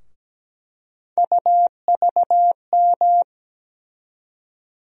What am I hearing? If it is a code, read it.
Morse code, heard as a keyed tone, about 17 wpm, 700 Hz. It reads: UVM
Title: USA Call Sign Postfixes - 3 Letters 17wpm